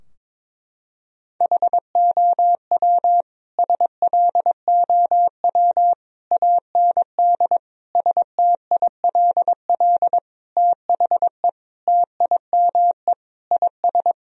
Morse code, heard as a keyed tone, about 22 wpm, 700 Hz. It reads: HOW SLOW AND STILL THE TIME IS
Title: Sentences from Top 1000 Words - Review Part 2 - 22wpm